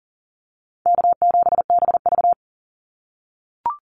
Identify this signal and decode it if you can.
Morse code, heard as a keyed tone, about 40 wpm, 700 Hz. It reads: K7BV
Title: Call Signs - 40wpm